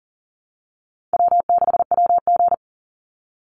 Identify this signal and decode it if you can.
Morse code, heard as a keyed tone, about 40 wpm, 700 Hz. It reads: W6WG
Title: Call Signs Repeated 2x - 40wpm